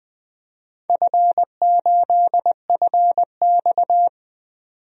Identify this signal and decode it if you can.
Morse code, heard as a keyed tone, about 20 wpm, 700 Hz. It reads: F8FX
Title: DXCC-100 Call Signs - 4 Characters - 20wpm